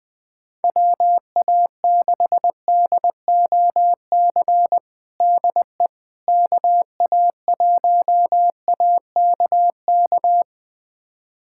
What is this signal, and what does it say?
Morse code, heard as a keyed tone, about 21 wpm, 700 Hz. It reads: WA6DOC DE KA1AKK
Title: USA Callsign DE Callsign - 6 Characters - 20wpm